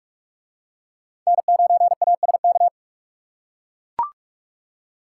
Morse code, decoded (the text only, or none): N9ASK